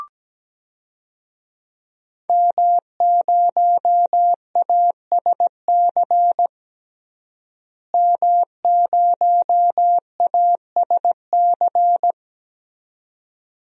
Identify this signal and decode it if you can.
Morse code, heard as a keyed tone, about 17 wpm, 700 Hz. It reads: M0ASC M0ASC
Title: DXCC-100 Call Signs - 5 Characters - Repeated 2x - 17wpm